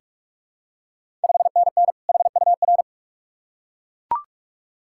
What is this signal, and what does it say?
Morse code, heard as a keyed tone, about 46 wpm, 690 Hz. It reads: 5NN HUR
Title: Ontario QSO Party Exchange 45wpm